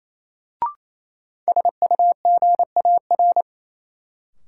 Morse code, heard as a keyed tone, about 28 wpm, 700 Hz. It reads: SUGAR
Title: Sentences from Top 1200 Words - New Word Drill - 28wpm